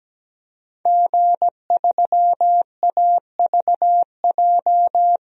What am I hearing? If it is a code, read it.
Morse code, heard as a keyed tone, about 17 wpm, 700 Hz. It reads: G3AVJ
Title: DXCC-100 Call Signs - 5 Characters - Repeated 2x - 17wpm